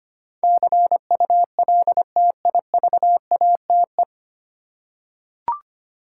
Morse code, heard as a keyed tone, about 25 wpm, 700 Hz. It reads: CULTIVATE
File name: practice / 25wpm / 9 Letter Words - First 3 Letters Given - 25wpm